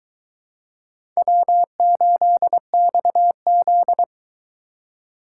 Morse code, heard as a keyed tone, about 22 wpm, 700 Hz. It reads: W8XZ